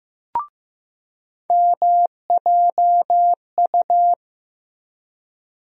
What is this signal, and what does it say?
Morse code, heard as a keyed tone, about 15 wpm, 700 Hz. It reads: MJU
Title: USA Call Sign Postfixes - 3 Letters 15wpm